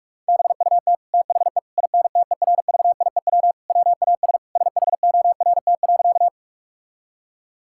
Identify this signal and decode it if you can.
Morse code, heard as a keyed tone, about 46 wpm, 700 Hz. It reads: BUT THE INTERVIEW WAS SHORT.
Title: Sentences from Top 1600 Words 45wpm